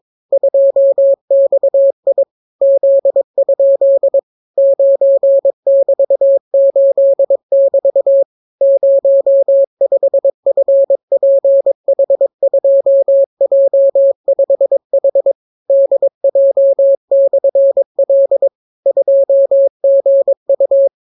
Morse code, heard as a keyed tone, about 22 wpm, 550 Hz. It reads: 2XI Z? 9<BT>8<BT> 05FPH2J5H DJ/L 2GU